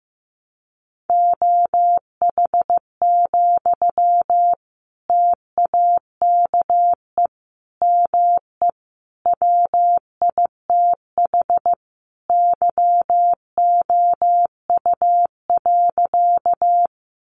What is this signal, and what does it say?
Morse code, heard as a keyed tone, about 15 wpm, 700 Hz. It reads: OH, TAKE ME WITH YOU.
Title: Sentences from Top 500 Words 15wpm